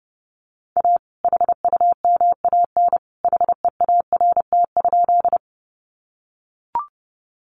Morse code, heard as a keyed tone, about 30 wpm, 700 Hz. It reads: A HUMAN HEART?
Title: Sentences from Top 1200 Words 30wpm